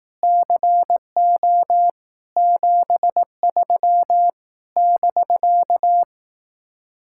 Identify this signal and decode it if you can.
Morse code, heard as a keyed tone, about 18 wpm, 700 Hz. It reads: CO 73 <BK>
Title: Speed-Racing Common QSO Phrases Encore - 15wpm